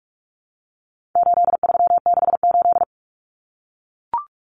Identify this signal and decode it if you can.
Morse code, heard as a keyed tone, about 44 wpm, 700 Hz. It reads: 8368
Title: Numbers - 4-Digit 45wpm